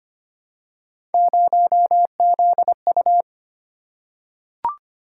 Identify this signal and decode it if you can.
Morse code, heard as a keyed tone, about 25 wpm, 700 Hz. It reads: 0ZU
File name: Three Letter-Number Combinations - 25wpm